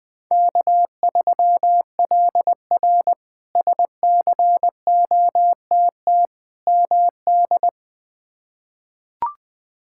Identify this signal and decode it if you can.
Morse code, heard as a keyed tone, about 20 wpm, 700 Hz. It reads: K3LR SCOTT MD
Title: K1USN SST QSO Practice - 20wpm